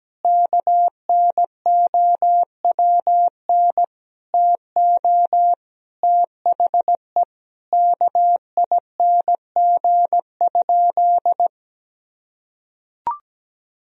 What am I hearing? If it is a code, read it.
Morse code, heard as a keyed tone, about 17 wpm, 700 Hz. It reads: KNOWN TO THE KING?
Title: Sentences from Top 1000 Words - Review Part 4 - 17wpm